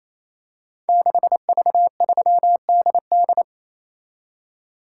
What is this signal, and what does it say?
Morse code, heard as a keyed tone, about 27 wpm, 700 Hz. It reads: 6V3DD
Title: Code Groups - 5 Letter:Numbers - 28wpm